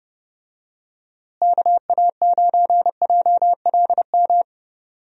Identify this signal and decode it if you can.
Morse code, heard as a keyed tone, about 32 wpm, 700 Hz. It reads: KA9JLM
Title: USA Call Signs - 6 Characters - Repeated 2x - 30wpm